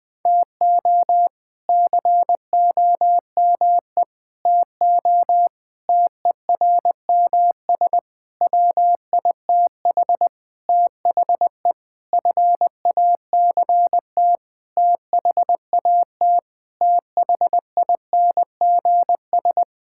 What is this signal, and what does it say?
Morse code, heard as a keyed tone, about 20 wpm, 700 Hz. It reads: TO COME TO TERMS WITH THE FACT THAT THINGS
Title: Sentences with 8-Word Phrases - 20wpm